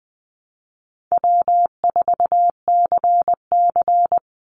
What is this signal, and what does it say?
Morse code, heard as a keyed tone, about 20 wpm, 700 Hz. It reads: W4CC
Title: Easy Call Signs Repeated 3x - 20wpm